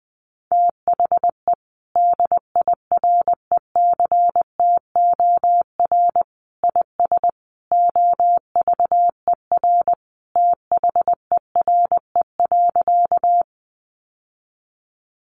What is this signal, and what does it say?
Morse code, heard as a keyed tone, about 20 wpm, 700 Hz. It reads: THE DIRECTOR IS OVER THERE.
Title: Sentences from Top 1300 Words 20wpm